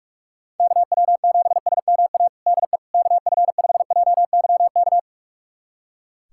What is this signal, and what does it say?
Morse code, heard as a keyed tone, about 45 wpm, 700 Hz. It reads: KW7SMA DE KF5JYK
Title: USA Callsign DE Callsign 45wpm